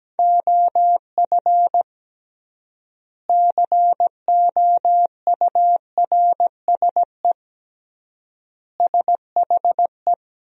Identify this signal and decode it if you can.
Morse code, heard as a keyed tone, about 17 wpm, 700 Hz. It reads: OF COURSE SHE
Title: 3x Word-Spacing - Sentences from Top 500 Words - 17wpm